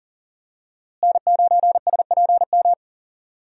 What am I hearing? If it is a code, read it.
Morse code, heard as a keyed tone, about 40 wpm, 700 Hz. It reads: N9SPM
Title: USA Call Signs - 5 Characters - Repeated 2x - 40wpm